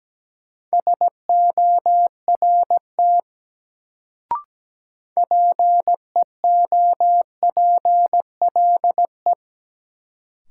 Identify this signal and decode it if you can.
Morse code, heard as a keyed tone, about 17 wpm, 700 Hz. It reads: SORT PEOPLE
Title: Sets of 2 Words - New Word Drill - 17wpm